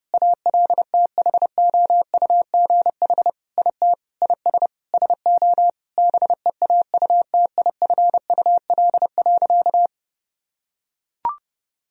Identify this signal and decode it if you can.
Morse code, heard as a keyed tone, about 30 wpm, 700 Hz. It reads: ALTHOUGH IT IS SO BEAUTIFUL.
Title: Sentences from Top 1000 Words - Review Part 3 - 30wpm